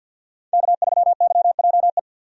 Morse code, heard as a keyed tone, about 47 wpm, 700 Hz. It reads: K3YJE
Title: USA Call Signs - 5 Characters - 50wpm